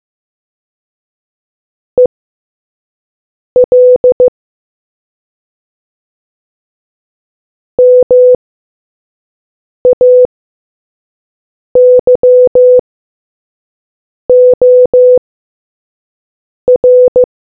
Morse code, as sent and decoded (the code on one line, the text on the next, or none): . .-.. / -- .- -.-- --- .-.
EL MAYOR